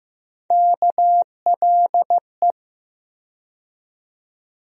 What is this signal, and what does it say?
Morse code, heard as a keyed tone, about 15 wpm, 700 Hz. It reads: KLE
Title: USA Call Sign Postfixes - 3 Letters 15wpm